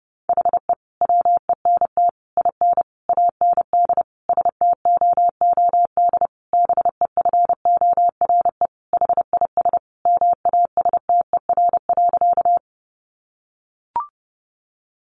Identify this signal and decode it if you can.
Morse code, heard as a keyed tone, about 30 wpm, 700 Hz. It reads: HE WENT IN AND STOOD BEFORE HIS MASTER.
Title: Sentences from Top 1300 Words 30wpm